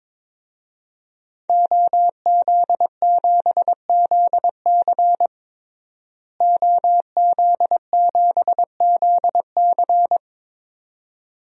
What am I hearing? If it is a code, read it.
Morse code, heard as a keyed tone, about 23 wpm, 700 Hz. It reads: OZ7ZC OZ7ZC
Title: DXCC-100 Call Signs - 5 Characters - Repeated 2x - 22wpm